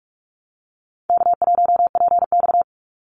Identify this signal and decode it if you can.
Morse code, heard as a keyed tone, about 44 wpm, 700 Hz. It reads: K1PX
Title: Call Signs Repeated 3x - 45wpm